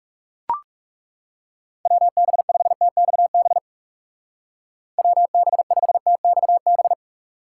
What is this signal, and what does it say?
Morse code, heard as a keyed tone, about 37 wpm, 700 Hz. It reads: WB5TXB WB5TXB
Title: Speed Racing - USA 2x3 Call Signs 30wpm